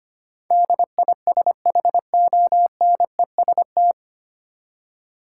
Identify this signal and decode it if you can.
Morse code, heard as a keyed tone, about 25 wpm, 700 Hz. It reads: DISHONEST
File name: Nine Letter Words - 25wpm